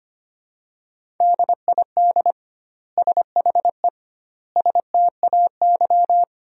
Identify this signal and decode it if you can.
Morse code, heard as a keyed tone, about 25 wpm, 700 Hz. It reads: DID SHE STAY
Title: 2x Word-Spacing - Sentences from Top 500 Words - 25wpm